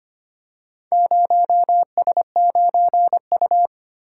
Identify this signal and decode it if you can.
Morse code, heard as a keyed tone, about 25 wpm, 700 Hz. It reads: 0S9U
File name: practice / 25wpm / Four Letter-Number Combinations - 25wpm